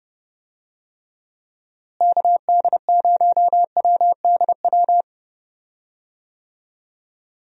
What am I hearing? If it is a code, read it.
Morse code, heard as a keyed tone, about 32 wpm, 700 Hz. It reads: KD0WDW
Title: USA Call Signs - 6 Characters - Repeated 3x - 30wpm